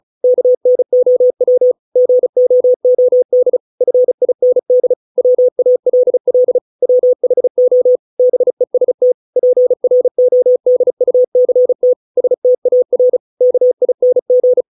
Morse code, heard as a keyed tone, about 35 wpm, 500 Hz. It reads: KNOW GOOD FIND WALL WHO BEST PRODUCT STAR KING